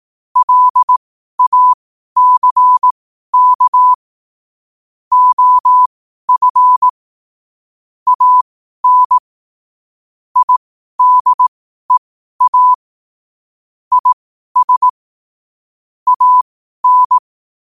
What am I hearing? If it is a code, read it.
Morse code, heard as a keyed tone, about 18 wpm, spaced as for 12 wpm, 1000 Hz. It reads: LACK OF AN IDEA IS AN
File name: qotdfiles / 12wpm